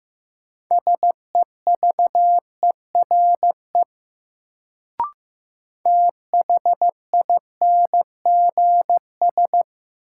Morse code, heard as a keyed tone, about 15 wpm, 700 Hz. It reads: SEVERE THINGS
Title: Sentences from Top 1400 Words - New Word Drill Alt - 15wpm